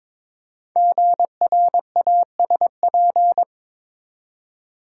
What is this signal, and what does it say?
Morse code, heard as a keyed tone, about 22 wpm, 700 Hz. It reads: GRASP
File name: 5 Letter Words - First 3 Letters Given - 22wpm